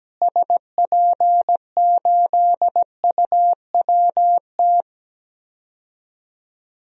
Morse code, heard as a keyed tone, about 17 wpm, 700 Hz. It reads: SP8UWT
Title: DXCC-100 Call Signs - 6 Characters - Repeated 2x - 17wpm